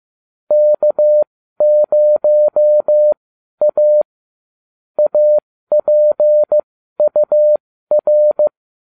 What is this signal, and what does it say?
Morse code, heard as a keyed tone, about 15 wpm, spaced as for 12 wpm, 600 Hz. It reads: K0A APUR